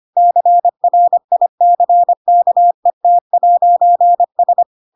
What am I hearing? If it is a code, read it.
Morse code, heard as a keyed tone, about 25 wpm, 700 Hz. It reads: CRICKET'S
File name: Pinocchio / Pinocchio ch5 25 25 ik1hge